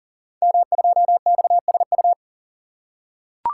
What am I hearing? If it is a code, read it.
Morse code, heard as a keyed tone, about 38 wpm, 700 Hz. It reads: M2XSU
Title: Code Groups - 5 Letter:Numbers - 40wpm